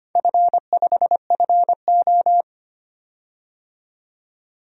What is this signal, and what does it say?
Morse code, heard as a keyed tone, about 25 wpm, 700 Hz. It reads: F5FO
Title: DXCC-100 Call Signs - 4 Characters - 25wpm